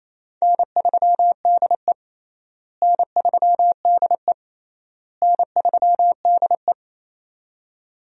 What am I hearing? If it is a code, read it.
Morse code, heard as a keyed tone, about 28 wpm, 700 Hz. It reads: N3DE N3DE N3DE
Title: USA Call Signs - 4 Characters - Repeated 3x - 28wpm